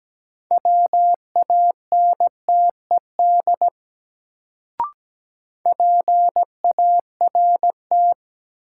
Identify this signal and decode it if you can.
Morse code, heard as a keyed tone, about 17 wpm, 700 Hz. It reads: WANTED PART
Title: Sets of 2 Words - New Word Drill - 17wpm